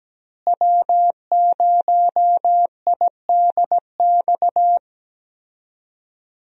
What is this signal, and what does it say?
Morse code, heard as a keyed tone, about 17 wpm, 700 Hz. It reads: W0IDX
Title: USA Call Signs - 5 Characters - Repeated 3x - 17wpm